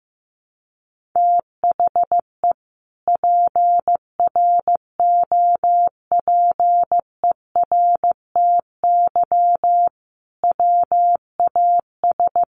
Morse code, heard as a keyed tone, about 15 wpm, 700 Hz. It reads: THE PROPERTY WAS
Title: Sentences from Top 1200 Words 15wpm